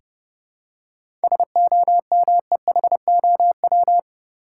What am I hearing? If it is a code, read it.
Morse code, heard as a keyed tone, about 30 wpm, 700 Hz. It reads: SOMEHOW
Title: Seven Letter Words - 30wpm